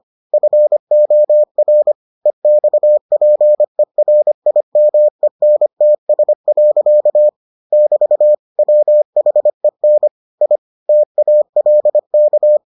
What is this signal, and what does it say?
Morse code, heard as a keyed tone, about 25 wpm, 600 Hz. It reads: FOR EXPERIMENTS. <BT> WHEN I TALK